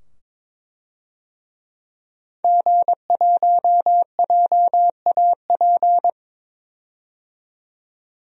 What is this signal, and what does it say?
Morse code, heard as a keyed tone, about 22 wpm, 700 Hz. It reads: G1JAP
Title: DXCC-100 Call Signs - 5 Characters - 22wpm